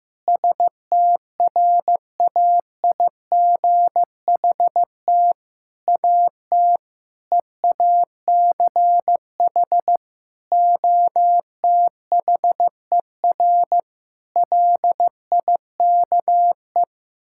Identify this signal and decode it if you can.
Morse code, heard as a keyed tone, about 15 wpm, 700 Hz. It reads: STRAIGHT AT EACH OTHER LIKE
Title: Sentences from Top 1000 Words - Review Part 6 - 15wpm